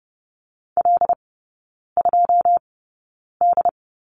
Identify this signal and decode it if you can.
Morse code, heard as a keyed tone, about 30 wpm, spaced as for 9 wpm, 700 Hz. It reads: L2D
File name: 081 Lesson 14e - Number 2 - Three Characters 30wpm